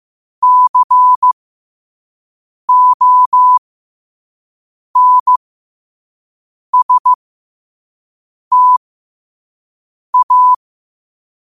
Morse code, heard as a keyed tone, about 15 wpm, spaced as for 5 wpm, 1000 Hz. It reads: CONSTA
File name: qotdfiles / 5wpm